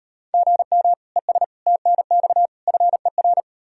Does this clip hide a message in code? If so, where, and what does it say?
Yes: Morse code from 0.3 to 3.4 s, heard as a keyed tone, about 38 wpm, 700 Hz. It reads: GM ES TNX FER